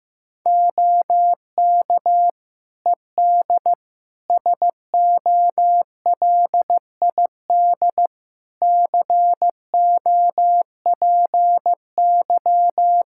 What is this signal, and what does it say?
Morse code, heard as a keyed tone, about 15 wpm, 700 Hz. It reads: OK ED SOLID COPY
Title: K1USN SST QSO Practice 15wpm